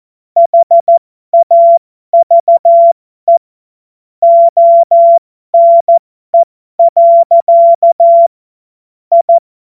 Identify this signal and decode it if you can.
Morse code, heard as a keyed tone, about 14 wpm, spaced as for 12 wpm, 680 Hz. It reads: HAVE ONE. I
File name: Morsekurs / Witze